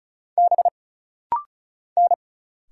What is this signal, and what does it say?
Morse code, heard as a keyed tone, about 34 wpm, 700 Hz. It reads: B N
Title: Single Letters - Rapid-Fire 35wpm